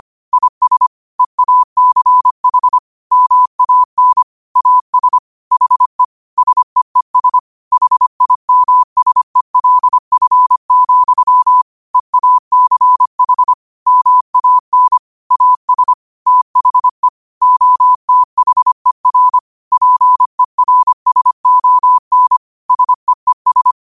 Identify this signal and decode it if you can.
Morse code, heard as a keyed tone, about 25 wpm, 1000 Hz. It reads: IS EACH MAN AS HE SEES HIMSELF, EACH MAN AS THE OTHER PERSON SEES